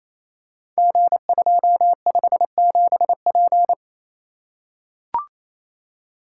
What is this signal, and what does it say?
Morse code, heard as a keyed tone, about 27 wpm, 700 Hz. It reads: G257P
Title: Code Groups - 5 Letter:Numbers - 28wpm